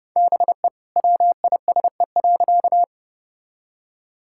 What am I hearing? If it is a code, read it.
Morse code, heard as a keyed tone, about 32 wpm, 700 Hz. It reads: BE WISE.
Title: Sentences from Top 1600 Words 30wpm